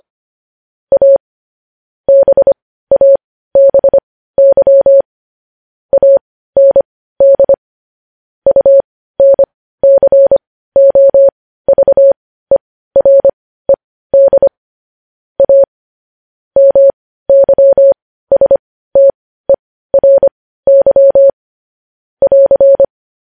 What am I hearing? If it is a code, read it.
Morse code, heard as a keyed tone, about 25 wpm, spaced as for 15 wpm, 550 Hz. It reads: A BABY AND UNCOVERED A MYSTERY <AR>
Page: News headlines at 15